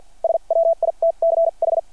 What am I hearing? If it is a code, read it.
Morse code, heard as a keyed tone, about 52 wpm, spaced as for 40 wpm, 660 Hz. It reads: SWITCH
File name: archived / CRSnet